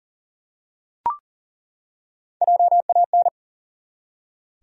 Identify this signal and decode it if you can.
Morse code, heard as a keyed tone, about 40 wpm, 700 Hz. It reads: JAN
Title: USA Common Names - Max 3 Letters - 40wpm